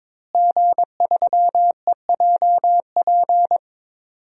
Morse code, heard as a keyed tone, about 23 wpm, 700 Hz. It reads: G3EJP
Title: DXCC-100 Call Signs - 5 Characters - Repeated 3x - 22wpm